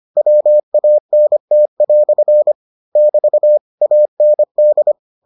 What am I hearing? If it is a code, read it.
Morse code, heard as a keyed tone, about 25 wpm, 600 Hz. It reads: WANT" <BT> AND